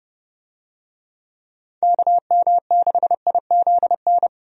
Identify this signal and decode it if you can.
Morse code, heard as a keyed tone, about 30 wpm, 700 Hz. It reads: KM6IZN
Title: USA Call Signs - 6 Characters - Repeated 3x - 30wpm